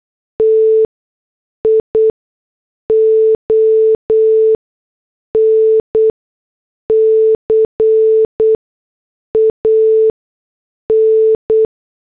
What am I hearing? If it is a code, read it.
Morse code, heard as a keyed tone, about 8 wpm, spaced as for 6 wpm, 440 Hz. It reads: TION CAN